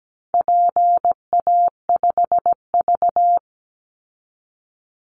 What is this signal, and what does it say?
Morse code, heard as a keyed tone, about 17 wpm, 700 Hz. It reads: PA5V
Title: Call Signs Repeated 3x - 17wpm